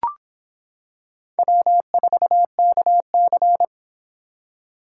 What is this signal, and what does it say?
Morse code, heard as a keyed tone, about 26 wpm, 700 Hz. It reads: W4KC